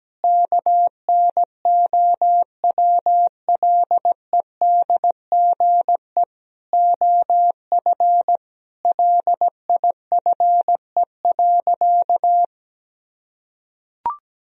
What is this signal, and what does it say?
Morse code, heard as a keyed tone, about 17 wpm, 700 Hz. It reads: KNOWLEDGE OF LIFE.
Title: Sentences from Top 1000 Words - Review Part 3 - 17wpm